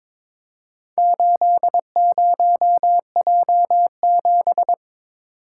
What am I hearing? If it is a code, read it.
Morse code, heard as a keyed tone, about 22 wpm, 700 Hz. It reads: 80J7
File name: practice / 22wpm / Four Letter-Number Combinations - 22wpm